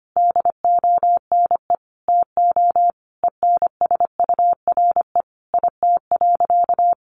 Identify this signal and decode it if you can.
Morse code, heard as a keyed tone, about 25 wpm, 700 Hz. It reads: DONE TO ENSURE IT.
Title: Sentences from Top 1100 Words 25wpm